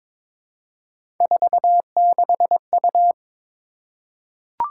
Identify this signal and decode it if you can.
Morse code, heard as a keyed tone, about 22 wpm, 700 Hz. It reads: 46U
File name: practice / 22wpm / Three Letter-Number Combinations - 22wpm